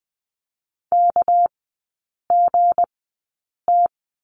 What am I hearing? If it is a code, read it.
Morse code, heard as a keyed tone, about 20 wpm, spaced as for 8 wpm, 700 Hz. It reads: KGT
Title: 198 Lesson 32e - Letter k - Three Characters 20wpm